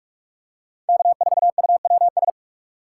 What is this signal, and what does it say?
Morse code, heard as a keyed tone, about 44 wpm, 690 Hz. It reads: K4UWS
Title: USA Call Signs - 5 Characters - 6x Character Spacing - 45wpm